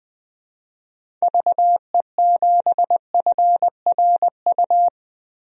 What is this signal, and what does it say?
Morse code, heard as a keyed tone, about 20 wpm, 700 Hz. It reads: VE7FRU
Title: DXCC-100 Call Signs - 6 Characters - 20wpm